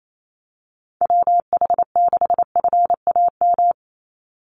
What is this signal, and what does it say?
Morse code, heard as a keyed tone, about 28 wpm, 700 Hz. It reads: WH6FAM
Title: USA Call Signs 28wpm